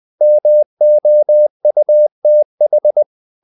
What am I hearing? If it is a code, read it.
Morse code, heard as a keyed tone, about 20 wpm, 600 Hz. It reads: MOUTH
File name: the old man and the sea / the old man and the sea part 1 20wpm